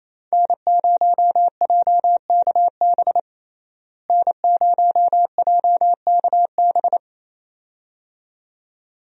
Morse code, heard as a keyed tone, about 28 wpm, 700 Hz. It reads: N0JKB N0JKB
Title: USA Call Signs - 5 Characters - Repeated 2x - 28wpm